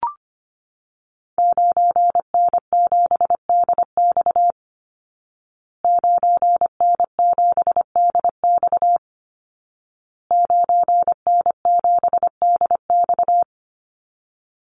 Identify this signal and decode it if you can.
Morse code, heard as a keyed tone, about 25 wpm, 700 Hz. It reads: 9N7DX 9N7DX 9N7DX
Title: Call Signs Repeated 2x - 25wpm